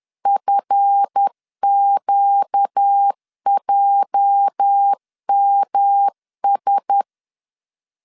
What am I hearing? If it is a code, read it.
Morse code, heard as a keyed tone, about 10 wpm, 800 Hz. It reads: FQJMS
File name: LEZIONI / 10 SIGNS / 10A SLASH 10wpm 2808